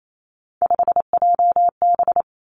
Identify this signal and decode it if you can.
Morse code, heard as a keyed tone, about 28 wpm, 700 Hz. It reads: 5JB
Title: Three Challenging Letter-Number Combinations 28wpm